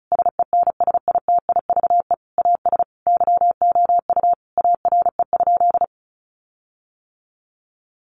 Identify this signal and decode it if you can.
Morse code, heard as a keyed tone, about 34 wpm, 700 Hz. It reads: SENSITIVE AS YOU ARE?
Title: Sentences from Top 1400 Words 35wpm